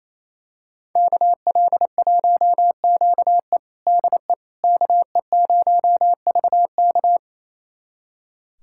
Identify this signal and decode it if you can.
Morse code, heard as a keyed tone, about 28 wpm, 700 Hz. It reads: KL1QE DE KE0VK
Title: USA Callsign DE Callsign - 5 Characters - 28wpm